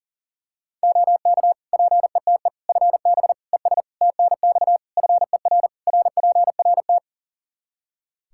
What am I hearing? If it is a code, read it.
Morse code, heard as a keyed tone, about 40 wpm, 700 Hz. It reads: OK PETE FB ES TNX FER RPRT
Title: Practice QSO - Exchange 2 - Info - 40wpm